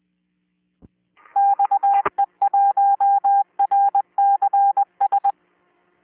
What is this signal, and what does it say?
Morse code, heard as a keyed tone, about 21 wpm, 800 Hz. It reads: XE1RCS